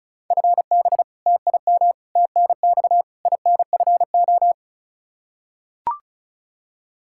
Morse code, heard as a keyed tone, about 36 wpm, 700 Hz. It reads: FB TIM TNX INFO
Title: K1USN SST QSO Practice 35wpm